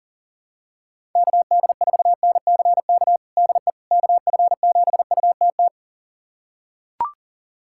Morse code, heard as a keyed tone, about 40 wpm, 700 Hz. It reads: KD4NCK DE KF7UTT
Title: USA Callsign DE Callsign 40wpm